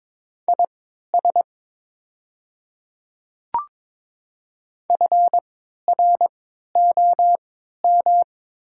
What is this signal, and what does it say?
Morse code, heard as a keyed tone, about 22 wpm, spaced as for 12 wpm, 700 Hz. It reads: IS FROM
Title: 3x Character Spacing - Top 100 Words - 22wpm